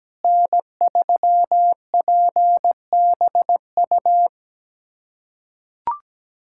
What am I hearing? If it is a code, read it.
Morse code, heard as a keyed tone, about 17 wpm, 700 Hz. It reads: N3PBU
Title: USA Callsign DE Callsign - 5 Characters - 17wpm